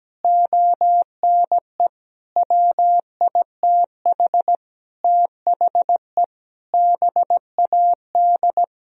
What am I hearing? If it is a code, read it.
Morse code, heard as a keyed tone, about 17 wpm, 700 Hz. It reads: ONE WITH THE BAD
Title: Sentences from Top 1000 Words - Review Part 3 - 17wpm